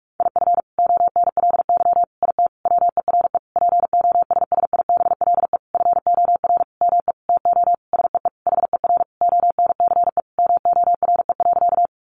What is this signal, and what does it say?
Morse code, heard as a keyed tone, about 46 wpm, 700 Hz. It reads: IF ONLY IT WERE POSSIBLE FOR ME TO SEE HER ONCE MORE.
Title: Sentences from Top 500 Words 45wpm